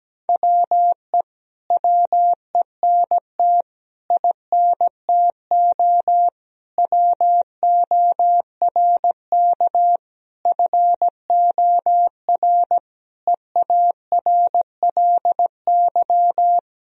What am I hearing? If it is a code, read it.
Morse code, heard as a keyed tone, about 17 wpm, 700 Hz. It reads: WE WENT INTO WORK FOR EARLY
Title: Sentences from Top 1000 Words - Review Part 4 - 17wpm